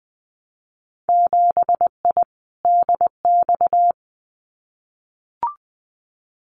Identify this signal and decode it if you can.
Morse code, heard as a keyed tone, about 20 wpm, 700 Hz. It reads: 7I DX
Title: Winter Field Day Exchanges Encore 20wpm